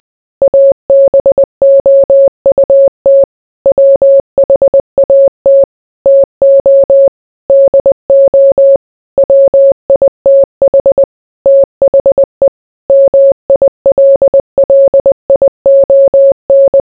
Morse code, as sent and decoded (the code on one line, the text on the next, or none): .- -... --- ..- - / .-- .... .- - / - --- / -.. --- / .-- .. - .... / - .... / -- .. .-.. .-.. .. --- -.
ABOUT WHAT TO DO WITH THE MILLION